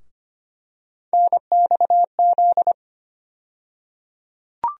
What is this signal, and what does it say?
Morse code, heard as a keyed tone, about 25 wpm, 700 Hz. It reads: NXZ
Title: USA Call Sign Postfixes - 3 Letters 25wpm